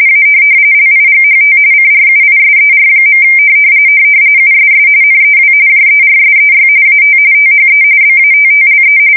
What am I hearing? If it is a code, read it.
RTTY, about 45.45 baud. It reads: A LIGHT CRUISER. USS NEW JERSEY AT THE BATTLE
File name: fsk170 / USS-New-Jersey